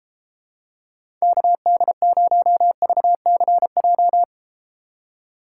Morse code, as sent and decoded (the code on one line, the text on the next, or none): -.- -.. ----- ...- -.-. .---
KD0VCJ